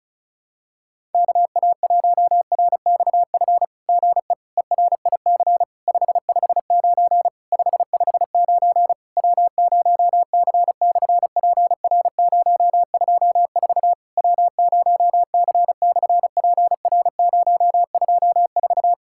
Morse code, heard as a keyed tone, about 36 wpm, 700 Hz. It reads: KA1RXF GE ERIC 559 559 W0C/PR024 W0C/PR024
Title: SOTA QSO Practice 35wpm